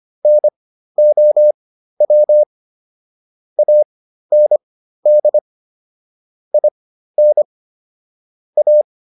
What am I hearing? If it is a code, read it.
Morse code, heard as a keyed tone, about 25 wpm, spaced as for 13 wpm, 600 Hz. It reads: NOW AND IN A